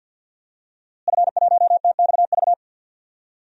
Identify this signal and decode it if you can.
Morse code, heard as a keyed tone, about 52 wpm, 700 Hz. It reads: F1TXV
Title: DXCC-100 Call Signs - 5 Characters - Repeated 2x - 50wpm